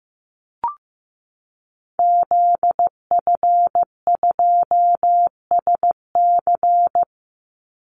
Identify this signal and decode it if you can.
Morse code, heard as a keyed tone, about 15 wpm, 700 Hz. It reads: ZF2SC
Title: Call Signs Repeated 3x - 15wpm